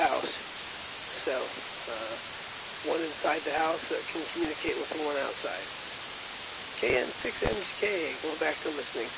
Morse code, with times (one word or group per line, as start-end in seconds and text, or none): none